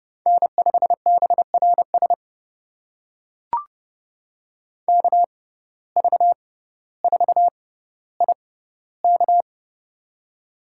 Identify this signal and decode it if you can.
Morse code, heard as a keyed tone, about 30 wpm, spaced as for 15 wpm, 700 Hz. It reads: N5BRS KV4IK
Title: USA Call Signs - 5 Characters - 6x Character Spacing - 30wpm